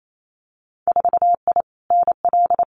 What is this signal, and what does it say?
Morse code, heard as a keyed tone, about 28 wpm, 700 Hz. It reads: 4I NL
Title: Winter Field Day Exchanges Encore 28wpm